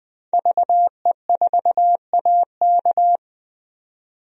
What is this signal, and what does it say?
Morse code, heard as a keyed tone, about 20 wpm, 700 Hz. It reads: VE4AK
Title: DXCC-100 Call Signs - 5 Characters - 20wpm